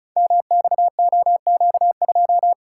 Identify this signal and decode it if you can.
Morse code, heard as a keyed tone, about 36 wpm, 700 Hz. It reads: MXOQ2
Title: Code Groups - 5 Letter:Numbers - 35wpm